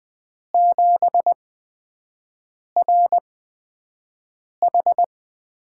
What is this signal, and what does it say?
Morse code, heard as a keyed tone, about 20 wpm, spaced as for 5 wpm, 700 Hz. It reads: 7RH
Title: USA Call Signs - 8x Character Spacing - 20wpm